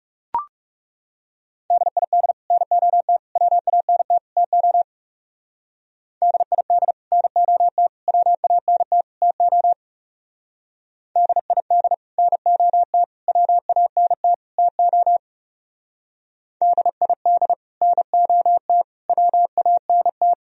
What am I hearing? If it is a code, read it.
Morse code, heard as a keyed tone, about 37 wpm, 700 Hz. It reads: DID NOT WANT TO DID NOT WANT TO DID NOT WANT TO DID NOT WANT
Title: Speed-Racing - Sets of 4 Words 30wpm